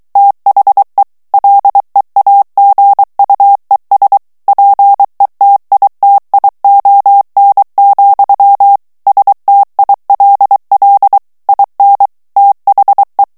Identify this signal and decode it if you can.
Morse code, heard as a keyed tone, about 23 wpm, 800 Hz. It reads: THE LEAGUES PETITION, STILL IN THE
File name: ANGLAIS texte clair 3 vitesse 22 mots minute